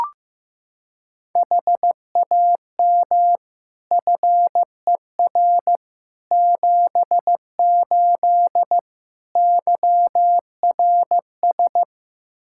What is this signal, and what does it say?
Morse code, heard as a keyed tone, about 15 wpm, 700 Hz. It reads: HAM FER 78 YRS
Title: Practice QSO - Exchange 3 - More Info - 15wpm